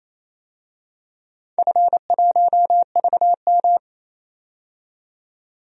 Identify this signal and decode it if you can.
Morse code, heard as a keyed tone, about 27 wpm, 700 Hz. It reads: F1VM
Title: DXCC-100 Call Signs - 4 Characters - 28wpm